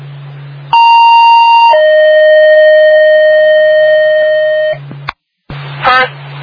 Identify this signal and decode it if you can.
Two-tone page, heard as a one-second tone, then a three-second tone, about 930 then 620 Hz, with noise. It is heard from 0.7 to 4.7 s.